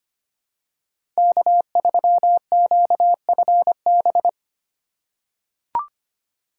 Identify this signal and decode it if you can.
Morse code, heard as a keyed tone, about 25 wpm, 700 Hz. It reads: K3QFB
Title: USA Call Signs - 5 Characters - 4x Character Spacing - 25wpm